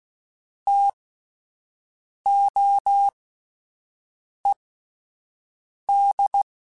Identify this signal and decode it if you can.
Morse code, heard as a keyed tone, about 16 wpm, spaced as for 5 wpm, 800 Hz. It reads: TOED